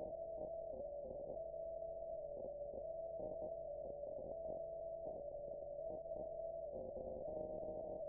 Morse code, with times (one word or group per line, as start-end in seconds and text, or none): none